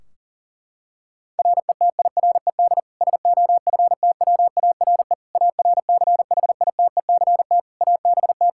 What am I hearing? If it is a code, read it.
Morse code, heard as a keyed tone, about 37 wpm, 700 Hz. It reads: RETIRED SOFTWARE ARCHITECT ABT
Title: Practice QSO - Exchange 3 - More Info - 40wpm